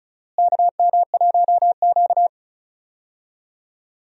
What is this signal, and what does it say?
Morse code, heard as a keyed tone, about 36 wpm, 700 Hz. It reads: KM1Q
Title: USA Call Signs - 4 Characters - Repeated 3x - 35wpm